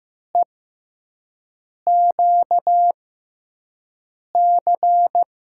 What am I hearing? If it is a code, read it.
Morse code, heard as a keyed tone, about 15 wpm, spaced as for 5 wpm, 700 Hz. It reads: EQC
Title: USA Call Signs - 6 Characters - 6x Character Spacing - 15wpm